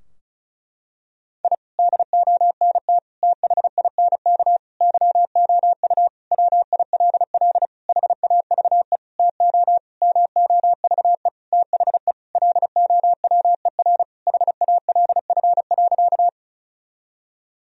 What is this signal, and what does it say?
Morse code, heard as a keyed tone, about 36 wpm, 700 Hz. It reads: I DONT THINK YOU WILL HAVE TO MOVE THE LOWER HALF.
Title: Sentences from Top 1000 Words - Review Part 4 - 35wpm